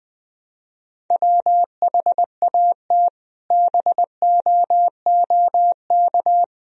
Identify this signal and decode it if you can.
Morse code, heard as a keyed tone, about 20 wpm, 700 Hz. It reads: WHAT BOOK
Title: Sentences from Top 1000 Words - Review Part 1 - 20wpm